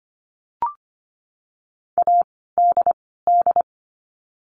Morse code, heard as a keyed tone, about 25 wpm, 700 Hz. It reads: A D D
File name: course / 25:10wpm / 094 Lesson 16f - Letter c - Words 25wpm